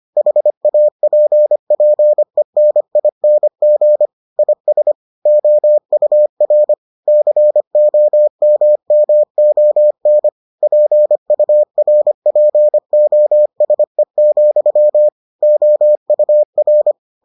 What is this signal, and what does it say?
Morse code, heard as a keyed tone, about 25 wpm, 600 Hz. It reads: HAPPENING IS OUR COMMON PURPOSE, OUR